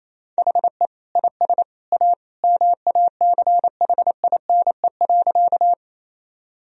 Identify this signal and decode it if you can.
Morse code, heard as a keyed tone, about 28 wpm, 700 Hz. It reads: HE IS A MACHINE.
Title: Sentences from Top 1000 Words - Review Part 2 - 28wpm